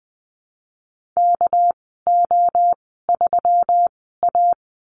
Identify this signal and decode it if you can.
Morse code, heard as a keyed tone, about 20 wpm, spaced as for 15 wpm, 700 Hz. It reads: KO3A
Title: 200 Lesson 32g - Letter k - Call Signs 20wpm